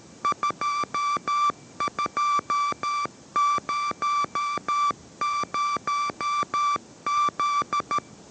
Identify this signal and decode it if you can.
Morse code, heard as a keyed tone, about 13 wpm, 1170 Hz. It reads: 2200Z